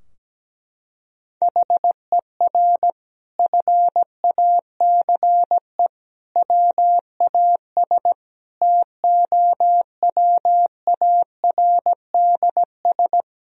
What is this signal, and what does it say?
Morse code, heard as a keyed tone, about 17 wpm, 700 Hz. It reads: HER FACE WAS TOWARDS
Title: Sentences from Top 1000 Words - Review Part 2 - 17wpm